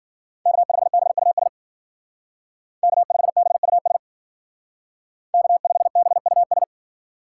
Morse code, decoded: K5BUS K5BUS K5BUS

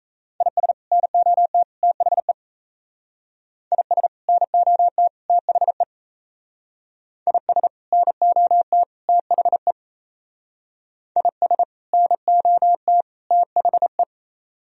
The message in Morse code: IS NOT THE IS NOT THE IS NOT THE IS NOT THE

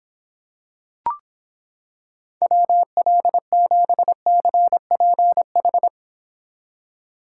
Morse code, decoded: WL7CPH

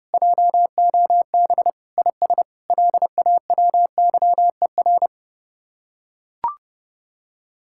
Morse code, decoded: JOB IS LAWYER

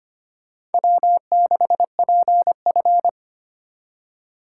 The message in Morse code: W6PF

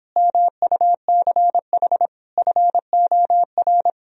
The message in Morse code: MUCH FOR

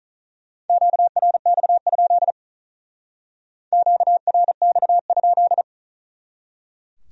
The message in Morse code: QRX? QRX?

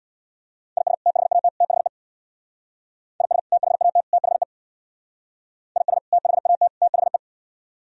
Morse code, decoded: IS THAT THE IS THAT THE IS THAT THE

